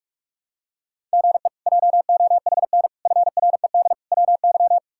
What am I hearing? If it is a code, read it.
Morse code, heard as a keyed tone, about 46 wpm, 700 Hz. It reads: GE JOHN FRED WY
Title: K1USN SST QSO Practice - 45wpm